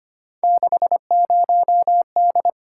Morse code, heard as a keyed tone, about 25 wpm, 700 Hz. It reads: 60D
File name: Three Letter-Number Combinations - 25wpm